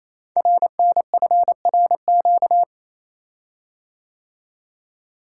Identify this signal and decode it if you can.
Morse code, heard as a keyed tone, about 29 wpm, 700 Hz. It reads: RNFRQ
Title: Code Groups - 5 Letters - 28wpm